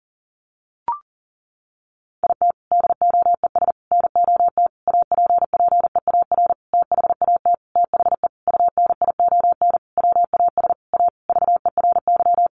Morse code, heard as a keyed tone, about 37 wpm, 700 Hz. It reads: IT DOES NOT APPEAR THAT THE UNION WAS A VERY
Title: Sentences from Top 1400 Words 40wpm